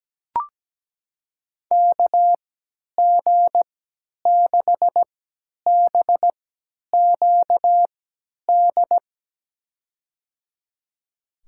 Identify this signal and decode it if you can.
Morse code, heard as a keyed tone, about 17 wpm, spaced as for 10 wpm, 700 Hz. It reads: KG6BQD
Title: USA Call Signs - 6 Characters - 3x Character Spacing - 17wpm